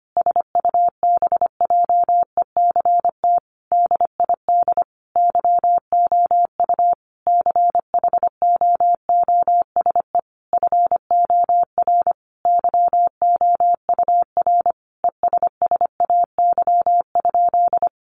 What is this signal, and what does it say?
Morse code, heard as a keyed tone, about 25 wpm, 700 Hz. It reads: SUBJECT DID YOU CHOOSE FOR YOUR ESSAY?
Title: Sentences from Top 1500 Words 25wpm